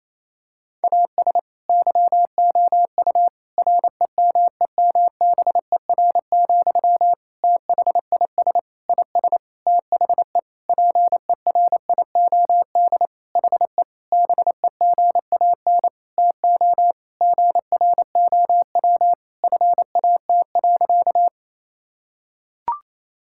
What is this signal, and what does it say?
Morse code, heard as a keyed tone, about 28 wpm, 700 Hz. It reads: AS YOU REMEMBER, THIS IS THE PERIOD HE BEGAN TO GROW FAT.
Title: Sentences from Top 1000 Words - Review Part 5 - 28wpm